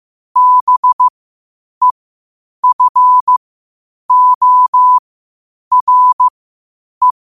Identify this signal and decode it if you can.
Morse code, heard as a keyed tone, about 15 wpm, spaced as for 8 wpm, 1000 Hz. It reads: BEFORE
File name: qotdfiles / 8wpm